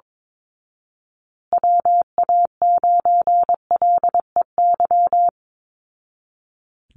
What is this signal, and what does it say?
Morse code, heard as a keyed tone, about 22 wpm, 700 Hz. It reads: WA9LEY
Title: Call Signs Repeated 2x - 22wpm